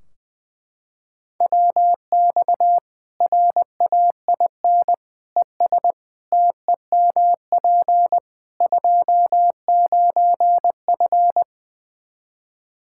Morse code, heard as a keyed tone, about 20 wpm, 700 Hz. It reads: WX RAIN ES TEMP 29F
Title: Practice QSO - Exchange 2 - Info - 20wpm